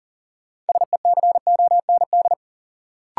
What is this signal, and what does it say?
Morse code, heard as a keyed tone, about 40 wpm, 700 Hz. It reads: SECOND
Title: Sentences from Top 300 Words - New Word Drill - 40wpm